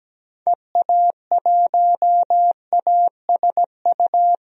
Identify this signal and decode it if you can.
Morse code, heard as a keyed tone, about 17 wpm, 700 Hz. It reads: EA1ASU
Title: DXCC-100 Call Signs - 6 Characters - 17wpm